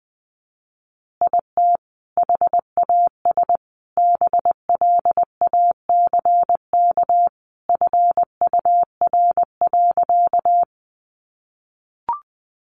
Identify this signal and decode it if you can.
Morse code, heard as a keyed tone, about 20 wpm, 700 Hz. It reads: IT HAS BLACK FUR.
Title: Easy Phrases 20wpm